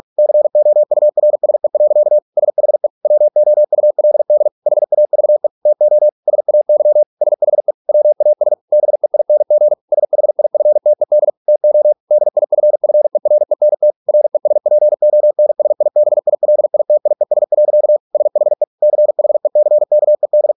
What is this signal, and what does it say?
Morse code, heard as a keyed tone, about 46 wpm, 600 Hz. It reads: COURSE. SHE WOULD HAVE TO SAY SHE WAS BEING SHIFTED TO DIFFERENT RESPONSIBILITIES. SHE CHECKED